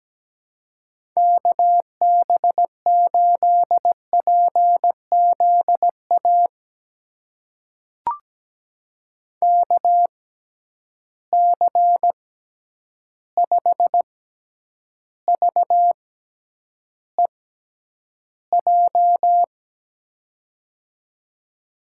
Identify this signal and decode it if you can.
Morse code, heard as a keyed tone, about 17 wpm, spaced as for 9 wpm, 700 Hz. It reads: KB8PZA KC5VEJ
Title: USA Call Signs - 6 Characters - 6x Character Spacing - 17wpm